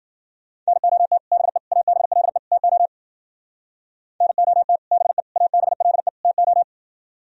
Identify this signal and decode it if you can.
Morse code, heard as a keyed tone, about 52 wpm, 700 Hz. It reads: NOT BE ABLE TO NOT BE ABLE TO